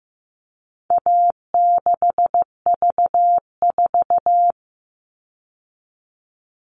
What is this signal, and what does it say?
Morse code, heard as a keyed tone, about 15 wpm, 700 Hz. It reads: A6V4
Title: All Cut Numbers - 4-Digit 15wpm